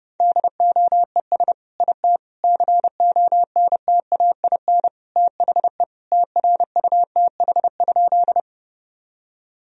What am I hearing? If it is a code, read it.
Morse code, heard as a keyed tone, about 30 wpm, 700 Hz. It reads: DOES IT CONTAIN THE TRUTH?
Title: Sentences from Top 1000 Words - Review Part 2 - 30wpm